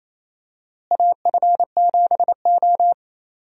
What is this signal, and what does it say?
Morse code, heard as a keyed tone, about 28 wpm, 700 Hz. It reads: AF7O
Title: USA Call Signs - 4 Characters - Repeated 2x - 28wpm